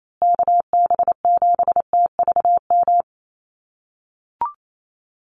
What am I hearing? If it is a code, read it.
Morse code, heard as a keyed tone, about 28 wpm, 700 Hz. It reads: KB7TVM
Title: USA Call Signs Repeated 3x 28wpm